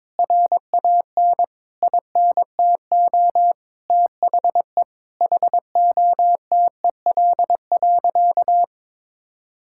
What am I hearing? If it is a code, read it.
Morse code, heard as a keyed tone, about 22 wpm, 700 Hz. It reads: RAN INTO THE HOTEL.
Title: Sentences from Top 1600 Words 22wpm